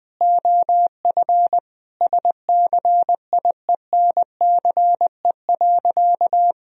Morse code, heard as a keyed tone, about 20 wpm, 700 Hz. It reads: OF SCIENCE.
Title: Sentences from Top 1000 Words - Review Part 3 - 20wpm